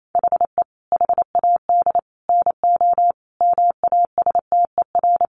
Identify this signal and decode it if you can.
Morse code, heard as a keyed tone, about 28 wpm, 700 Hz. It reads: HE HAD NO MASTER
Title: Sentences from Top 1300 Words 28wpm